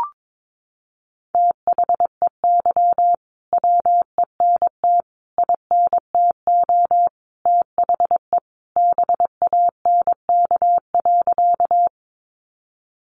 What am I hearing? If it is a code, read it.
Morse code, heard as a keyed tone, about 22 wpm, 700 Hz. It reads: THEY WENT INTO THE BANK.
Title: Sentences from Top 900 Words 22wpm